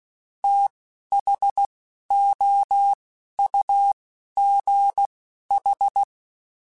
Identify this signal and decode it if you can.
Morse code, heard as a keyed tone, about 16 wpm, spaced as for 11 wpm, 800 Hz. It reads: THOUGH
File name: Technical Subjects / Technical Subjects-mp3s / Sw Pwr Sup @10